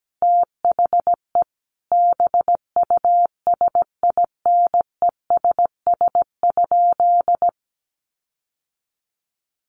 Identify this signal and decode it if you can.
Morse code, heard as a keyed tone, about 17 wpm, 700 Hz. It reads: THE BUSINESS?
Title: Sentences from Top 1400 Words 17wpm